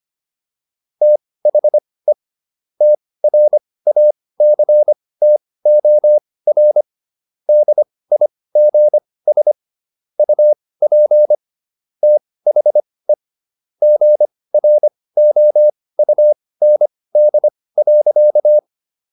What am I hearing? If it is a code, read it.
Morse code, heard as a keyed tone, about 25 wpm, spaced as for 18 wpm, 600 Hz. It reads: THE TRACTOR DIGS UP THE GROUND.